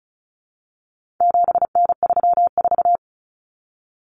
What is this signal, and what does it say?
Morse code, heard as a keyed tone, about 34 wpm, 700 Hz. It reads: 7N34
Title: Common Cut Numbers - 4-Digit 35wpm